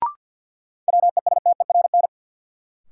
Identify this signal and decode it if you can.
Morse code, heard as a keyed tone, about 53 wpm, 700 Hz. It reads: WESTERN
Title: Sentences from Top 1000 Words - Review Part 7 - New Word Drill - 50wpm